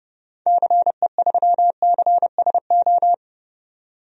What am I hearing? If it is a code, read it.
Morse code, heard as a keyed tone, about 30 wpm, 700 Hz. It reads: CE3CSO
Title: DXCC-100 Call Signs - 6 Characters - 30wpm